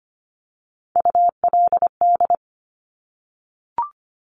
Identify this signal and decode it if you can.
Morse code, heard as a keyed tone, about 25 wpm, 700 Hz. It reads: ULD
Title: Top Three Letter Combinations in Words 25wpm